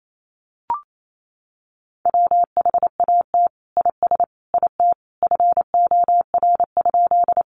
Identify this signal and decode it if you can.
Morse code, heard as a keyed tone, about 28 wpm, 700 Hz. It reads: WHAT IS IT FOR?
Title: Sentences from Top 100 Words 28wpm